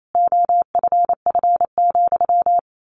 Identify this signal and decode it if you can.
Morse code, heard as a keyed tone, about 28 wpm, 700 Hz. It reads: OFF,
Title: Sentences from Top 1300 Words 28wpm